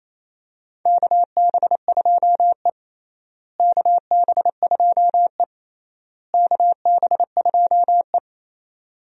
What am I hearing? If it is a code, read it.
Morse code, heard as a keyed tone, about 28 wpm, 700 Hz. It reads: KB2E KB2E KB2E
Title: USA Call Signs - 4 Characters - Repeated 3x - 28wpm